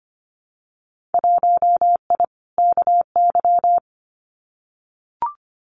Morse code, heard as a keyed tone, about 25 wpm, 690 Hz. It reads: 1I KY
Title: Winter Field Day Exchanges 25wpm